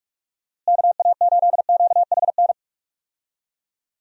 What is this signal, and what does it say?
Morse code, heard as a keyed tone, about 46 wpm, 700 Hz. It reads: KA8QHN
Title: USA Call Signs - 6 Characters - Repeated 3x - 45wpm